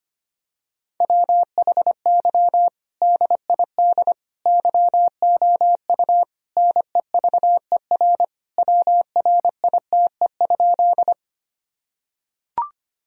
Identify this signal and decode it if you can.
Morse code, heard as a keyed tone, about 25 wpm, 700 Hz. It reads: WHY DID YOU NEVER WRITE?
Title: Sentences from Top 1000 Words - Review Part 1 - 25wpm